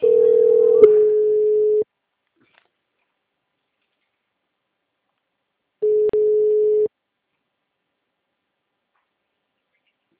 Ringback tone (Europe, 425 Hz).